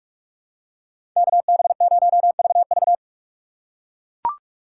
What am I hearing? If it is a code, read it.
Morse code, heard as a keyed tone, about 43 wpm, 700 Hz. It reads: KB0VV
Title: USA Call Signs - 5 Characters - 2x Character Spacing - 45wpm